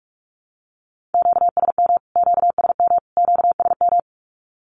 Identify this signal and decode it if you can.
Morse code, heard as a keyed tone, about 44 wpm, 700 Hz. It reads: QSM QSM QSM
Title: Q-Codes Repeated 3x 45wpm